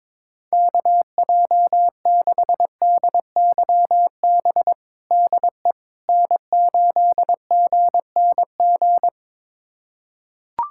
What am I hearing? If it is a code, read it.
Morse code, heard as a keyed tone, about 22 wpm, 700 Hz. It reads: KJ6DYB DE N8GNG
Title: USA Callsign DE Callsign 22wpm